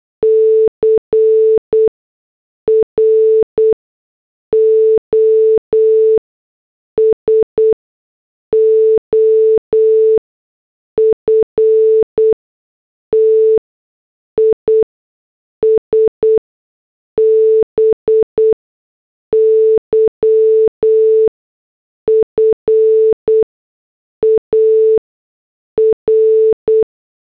Morse code, heard as a keyed tone, about 8 wpm, spaced as for 6 wpm, 440 Hz. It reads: CROSOFT IS BY FAR
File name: morse / headlines